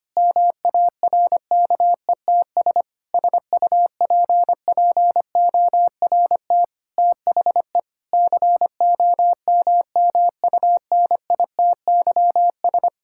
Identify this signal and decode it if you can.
Morse code, heard as a keyed tone, about 25 wpm, 700 Hz. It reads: MARKETS SUPPORT THE COMMUNITYS
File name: practice / 25wpm / Difficult Sentences with Hints - 25wpm